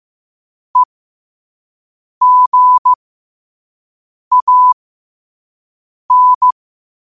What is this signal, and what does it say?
Morse code, heard as a keyed tone, about 15 wpm, spaced as for 5 wpm, 1000 Hz. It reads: EGAN
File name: qotdfiles / 5wpm